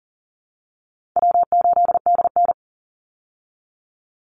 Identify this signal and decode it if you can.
Morse code, heard as a keyed tone, about 40 wpm, 700 Hz. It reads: W8DN
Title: Easy Call Signs Repeated 2x - 40wpm